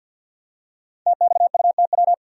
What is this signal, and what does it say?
Morse code, heard as a keyed tone, about 46 wpm, 700 Hz. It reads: TXUTW
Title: Code Groups - 5 Letters - 50wpm